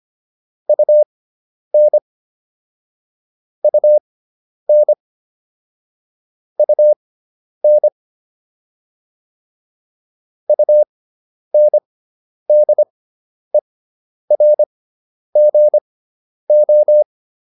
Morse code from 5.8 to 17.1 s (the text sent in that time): UN UNDERGO